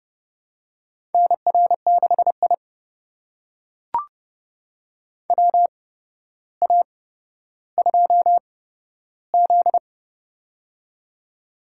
Morse code, heard as a keyed tone, about 30 wpm, spaced as for 13 wpm, 700 Hz. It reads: NR6I WA2Z
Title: USA Call Signs - 4 Characters - 8x Character Spacing - 30wpm